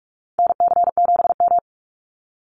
Morse code, heard as a keyed tone, about 44 wpm, 700 Hz. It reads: NC7M
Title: Call Signs Repeated 3x - 45wpm